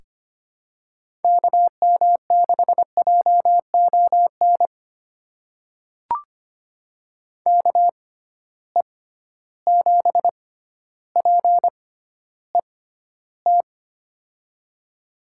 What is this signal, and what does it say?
Morse code, heard as a keyed tone, about 25 wpm, spaced as for 13 wpm, 700 Hz. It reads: KM6JON KE7PET
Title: USA Call Signs - 6 Characters - 6x Character Spacing - 25wpm